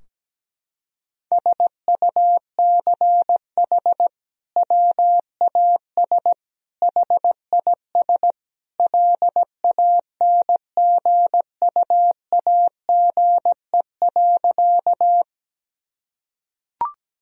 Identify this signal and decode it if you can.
Morse code, heard as a keyed tone, about 17 wpm, 700 Hz. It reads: SUCH WAS HIS LANGUAGE.
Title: Sentences from Top 1000 Words - Review Part 3 - 17wpm